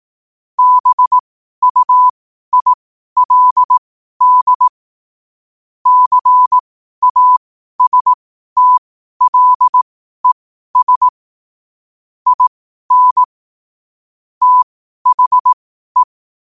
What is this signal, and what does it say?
Morse code, heard as a keyed tone, about 18 wpm, spaced as for 12 wpm, 1000 Hz. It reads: BUILD CASTLES IN THE
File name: qotdfiles / 12wpm